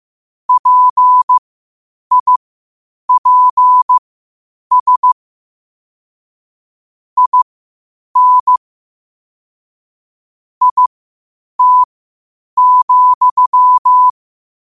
Morse code, heard as a keyed tone, about 15 wpm, spaced as for 8 wpm, 1000 Hz. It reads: PIPS IN IT,
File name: qotdfiles / 8wpm